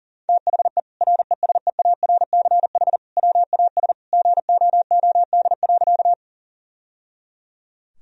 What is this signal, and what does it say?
Morse code, heard as a keyed tone, about 37 wpm, 700 Hz. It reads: THE RESEARCH WAS GOOD.
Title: Sentences from Top 1000 Words - Review Part 4 - 40wpm